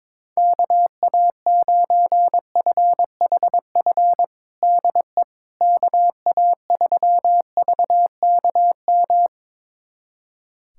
Morse code, heard as a keyed tone, about 22 wpm, 700 Hz. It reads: KA9FHF DE KA3VKM
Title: USA Callsign DE Callsign - 6 Characters - 22wpm